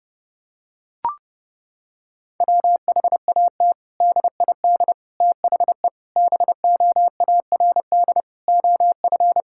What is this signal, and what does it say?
Morse code, heard as a keyed tone, about 30 wpm, 700 Hz. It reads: WHAT DID THE BOARD OF
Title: Sentences from Top 1000 Words - Review Part 6 - 30wpm